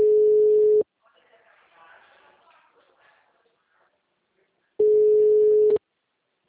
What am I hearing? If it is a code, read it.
Ringback tone (Europe, 425 Hz), with speech.